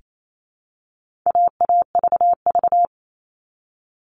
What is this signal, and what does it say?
Morse code, heard as a keyed tone, about 28 wpm, 700 Hz. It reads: AAVV
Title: All Cut Numbers - 4-Digit 28wpm